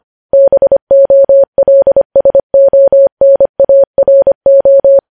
Morse code, heard as a keyed tone, about 25 wpm, 550 Hz. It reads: BOLSONARO
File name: podcast / News headlines-1x-25x25